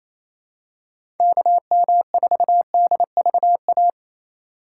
Morse code, heard as a keyed tone, about 28 wpm, 700 Hz. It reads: KM4DVA
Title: USA Call Signs - 6 Characters - 2x Character Spacing - 28wpm